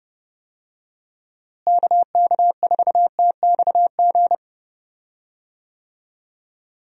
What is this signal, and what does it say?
Morse code, heard as a keyed tone, about 32 wpm, 700 Hz. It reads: KK4TXG
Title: USA Call Signs - 6 Characters - Repeated 2x - 30wpm